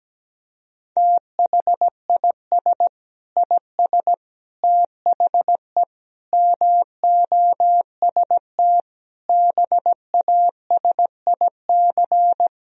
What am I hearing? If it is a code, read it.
Morse code, heard as a keyed tone, about 17 wpm, 700 Hz. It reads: THIS IS THE MOST BASIC
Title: Sentences from Top 1000 Words - Review Part 3 - 17wpm